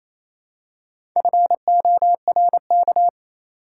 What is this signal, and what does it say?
Morse code, heard as a keyed tone, about 27 wpm, 700 Hz. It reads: FORK
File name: Easy Words - Speed-Racing with Hints - 22wpm